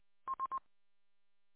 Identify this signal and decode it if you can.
There is no code.